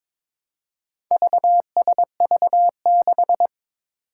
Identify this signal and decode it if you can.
Morse code, heard as a keyed tone, about 21 wpm, 700 Hz. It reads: VSV6
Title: Four Challenging Letter-Number Combinations 22wpm